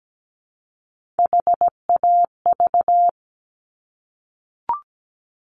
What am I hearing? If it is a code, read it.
Morse code, heard as a keyed tone, about 17 wpm, 700 Hz. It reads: HAV
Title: Top Three Letter Combinations in Words 17wpm